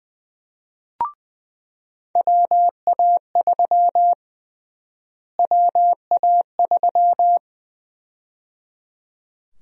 Morse code, heard as a keyed tone, about 20 wpm, 700 Hz. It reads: WA3 WA3
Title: USA Call Sign Prefixes - Repeated 2x - 20wpm